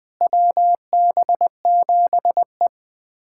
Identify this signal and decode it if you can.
Morse code, heard as a keyed tone, about 21 wpm, 700 Hz. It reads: WB7E
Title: USA Callsign DE Callsign - 4 Characters - 20wpm